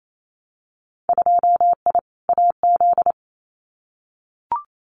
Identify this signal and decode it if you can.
Morse code, heard as a keyed tone, about 28 wpm, 700 Hz. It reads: 2I AZ
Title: Winter Field Day Exchanges Encore 28wpm